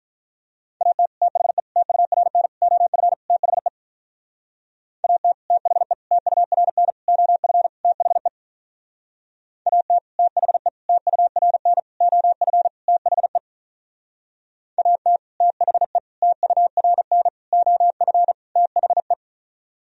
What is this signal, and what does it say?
Morse code, heard as a keyed tone, about 46 wpm, 700 Hz. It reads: AT THE TURN OF THE AT THE TURN OF THE AT THE TURN OF THE AT THE TURN OF THE